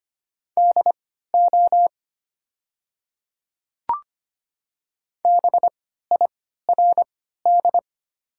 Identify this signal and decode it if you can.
Morse code, heard as a keyed tone, about 25 wpm, spaced as for 14 wpm, 700 Hz. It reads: DO BIRD
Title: 3x Character Spacing - Top 500 Words - 25wpm